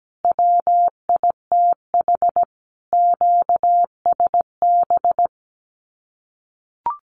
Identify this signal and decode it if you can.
Morse code, heard as a keyed tone, about 17 wpm, 700 Hz. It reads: WITH QSB
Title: Common QSO Phrases Encore 17wpm